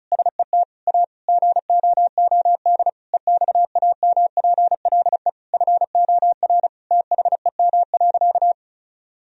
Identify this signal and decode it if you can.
Morse code, heard as a keyed tone, about 36 wpm, 690 Hz. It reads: SET A GOOD EXAMPLE FOR THEM.
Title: Sentences from Top 1000 Words - Review Part 1 - 35wpm